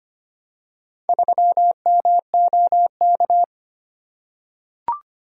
Morse code, heard as a keyed tone, about 25 wpm, 700 Hz. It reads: 3MOK
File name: Four Letter-Number Combinations - 25wpm